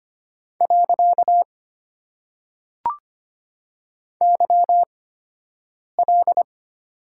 Morse code, heard as a keyed tone, about 25 wpm, spaced as for 7 wpm, 700 Hz. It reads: . YL
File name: Top 90 QSO Elements - 8x Character Spacing - 25wpm